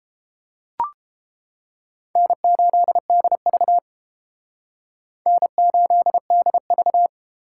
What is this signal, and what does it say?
Morse code, heard as a keyed tone, about 31 wpm, 700 Hz. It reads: N8DV N8DV